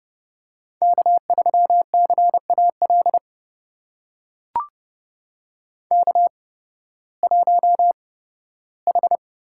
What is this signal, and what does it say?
Morse code, heard as a keyed tone, about 31 wpm, spaced as for 16 wpm, 700 Hz. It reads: K3CAL K1H
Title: USA Call Signs - 5 Characters - 8x Character Spacing - 30wpm